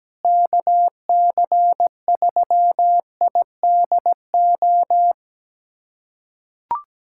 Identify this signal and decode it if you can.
Morse code, heard as a keyed tone, about 17 wpm, 700 Hz. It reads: KC3IDO
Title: USA Callsign DE Callsign 17wpm